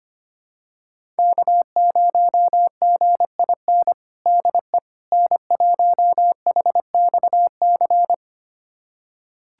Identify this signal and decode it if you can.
Morse code, heard as a keyed tone, about 25 wpm, 700 Hz. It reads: K0GIN DE N1HXC
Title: USA Callsign DE Callsign - 5 Characters - 25wpm